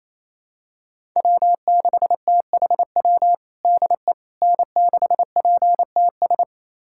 Morse code, heard as a keyed tone, about 28 wpm, 700 Hz. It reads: W6THW DE N6PTS
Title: USA Callsign DE Callsign - 5 Characters - 28wpm